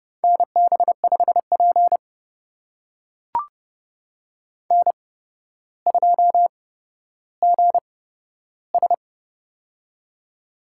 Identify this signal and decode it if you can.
Morse code, heard as a keyed tone, about 30 wpm, spaced as for 13 wpm, 700 Hz. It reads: NB5P N2GS
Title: USA Call Signs - 4 Characters - 8x Character Spacing - 30wpm